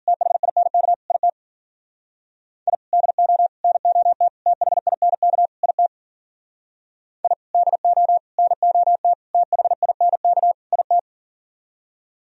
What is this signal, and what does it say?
Morse code, heard as a keyed tone, about 54 wpm, 700 Hz. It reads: THINK IT I DO NOT THINK IT I DO NOT THINK IT